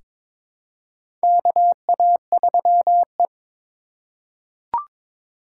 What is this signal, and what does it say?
Morse code, heard as a keyed tone, about 21 wpm, 700 Hz. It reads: KA3E
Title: USA Call Signs - 4 Characters - 4x Character Spacing - 22wpm